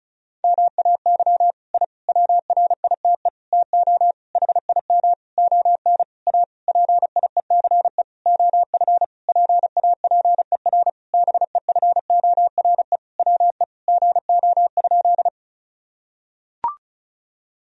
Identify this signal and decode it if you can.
Morse code, heard as a keyed tone, about 34 wpm, 700 Hz. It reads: MAY I WRITE TO HIM ON A PIECE OF PAPER BEFORE WE GO?
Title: Sentences from Top 1000 Words - Review Part 2 - 35wpm